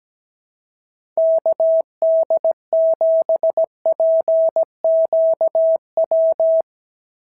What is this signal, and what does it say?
Morse code, heard as a keyed tone, about 17 wpm, 650 Hz. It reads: KD7PQW
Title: SOTA QSO Practice 17wpm